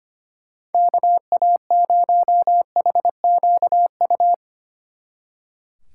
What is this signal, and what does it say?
Morse code, heard as a keyed tone, about 32 wpm, 700 Hz. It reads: KA0HQU